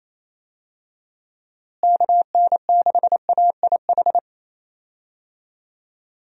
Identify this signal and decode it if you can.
Morse code, heard as a keyed tone, about 28 wpm, 700 Hz. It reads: KN6AIH
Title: USA Call Signs - 6 Characters - Repeated 2x - 28wpm